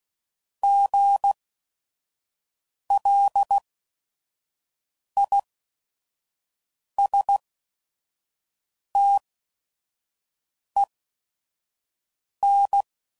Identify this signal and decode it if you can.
Morse code, heard as a keyed tone, about 16 wpm, spaced as for 5 wpm, 800 Hz. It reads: GLISTEN
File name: Prefixes / By First Letter / Gs